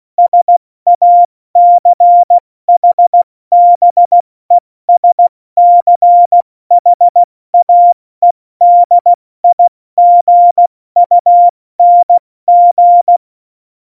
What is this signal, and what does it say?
Morse code, heard as a keyed tone, about 16 wpm, 700 Hz. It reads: SACHBESCHAEDIGUNG
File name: Morsekurs / Witze / Witz 05. Januar 2021, 70 F80